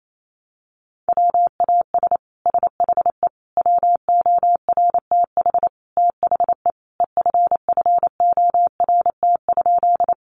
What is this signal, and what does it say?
Morse code, heard as a keyed tone, about 28 wpm, 700 Hz. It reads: WAS SHE WORTH THE EFFORT?
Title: Sentences from Top 1200 Words 28wpm